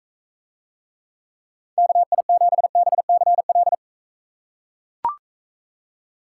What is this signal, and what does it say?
Morse code, heard as a keyed tone, about 42 wpm, 700 Hz. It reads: KI7BCL